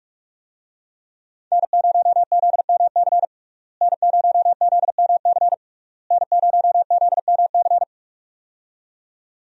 Morse code, heard as a keyed tone, about 46 wpm, 700 Hz. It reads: N0ZMC N0ZMC N0ZMC
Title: USA Call Signs - 5 Characters - Repeated 3x - 45wpm